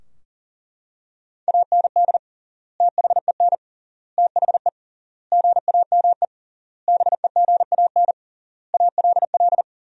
Morse code, heard as a keyed tone, about 40 wpm, 700 Hz. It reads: AND THEN THE GAME BEGAN ALL
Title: 3x Word-Spacing - Sentences from Top 500 Words - 40wpm